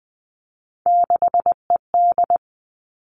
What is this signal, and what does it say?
Morse code, heard as a keyed tone, about 19 wpm, 700 Hz. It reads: 6ED
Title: All Cut Numbers - 3-Digit 20wpm